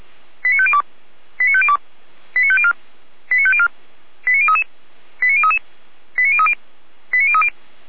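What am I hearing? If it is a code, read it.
Five-tone selective call (ZVEI), heard as quick runs of five tones, tones of about 70 ms. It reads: 89502 89502 89504 89504 89033 89033 89030 89030
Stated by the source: Five-tone ZVEI selective call received by SDR, archived by the fire brigade (Feuerwehr) Orlamünde.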